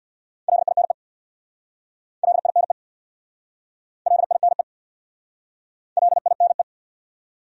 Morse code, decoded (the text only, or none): LINE LINE LINE LINE